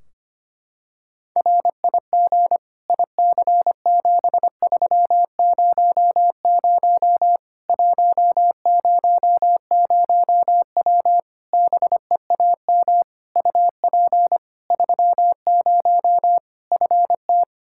Morse code, heard as a keyed tone, about 25 wpm, 700 Hz. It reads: RIG IC7300 100W BEAM UP 30 FT
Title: Practice QSO - Exchange 2 - Info - 25wpm